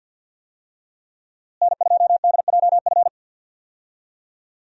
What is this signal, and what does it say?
Morse code, heard as a keyed tone, about 52 wpm, 700 Hz. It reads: N2DJF
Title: USA Call Signs - 5 Characters - Repeated 3x - 50wpm